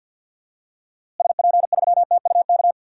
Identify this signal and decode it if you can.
Morse code, heard as a keyed tone, about 47 wpm, 700 Hz. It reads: SP3TUK
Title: DXCC-100 Call Signs - 6 Characters - Repeated 3x - 50wpm